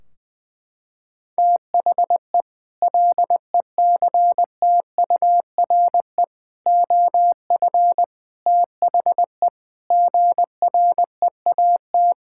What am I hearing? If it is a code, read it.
Morse code, heard as a keyed tone, about 20 wpm, 700 Hz. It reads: THE LECTURE OF THE GREAT
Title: Sentences from Top 1600 Words 20wpm